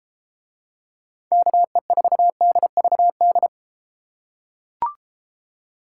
Morse code, heard as a keyed tone, about 33 wpm, 700 Hz. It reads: KE4DVD